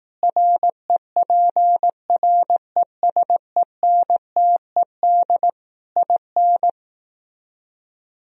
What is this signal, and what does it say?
Morse code, heard as a keyed tone, about 18 wpm, 700 Hz. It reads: REPRESENTED IN
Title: Speed-Racing - Sets of 5 Words 15wpm